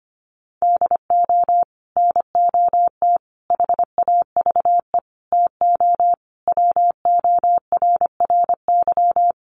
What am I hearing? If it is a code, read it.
Morse code, heard as a keyed tone, about 25 wpm, 700 Hz. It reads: DO NOT HAVE TO WORRY
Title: Sets of 5 Words 25wpm